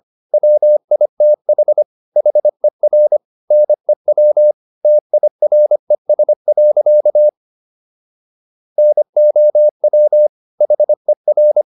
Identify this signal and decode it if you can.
Morse code, heard as a keyed tone, about 25 wpm, 600 Hz. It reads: WITH HER NEW TIRES. NOW HER